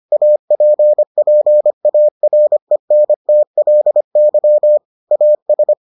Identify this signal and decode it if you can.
Morse code, heard as a keyed tone, about 25 wpm, 600 Hz. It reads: APPARENTLY AS